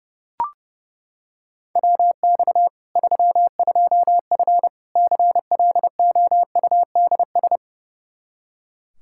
Morse code, heard as a keyed tone, about 30 wpm, 700 Hz. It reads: WX 32F CLOUDS
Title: Practice QSO - Exchange 2 - Info - 30wpm